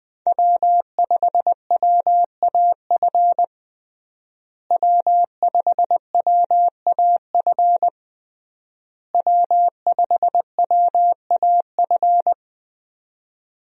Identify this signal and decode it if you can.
Morse code, heard as a keyed tone, about 20 wpm, 700 Hz. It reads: W5WAF W5WAF W5WAF
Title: USA Call Signs - 5 Characters - Repeated 3x - 20wpm